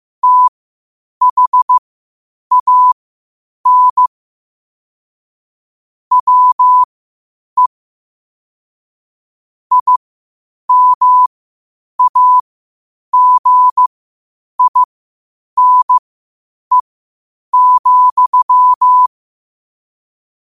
Morse code, heard as a keyed tone, about 15 wpm, spaced as for 8 wpm, 1000 Hz. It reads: THAN WE IMAGINE,
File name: qotdfiles / 8wpm